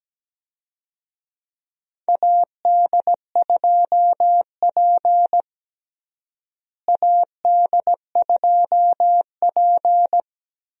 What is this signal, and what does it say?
Morse code, heard as a keyed tone, about 17 wpm, 700 Hz. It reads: AD2P AD2P
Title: USA Call Signs - 4 Characters - Repeated 3x - 17wpm